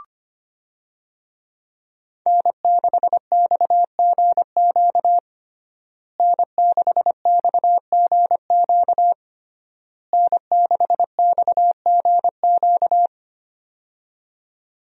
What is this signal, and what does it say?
Morse code, heard as a keyed tone, about 25 wpm, 700 Hz. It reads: N6XGQ N6XGQ N6XGQ
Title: USA Call Signs - 5 Characters - Repeated 3x - 25wpm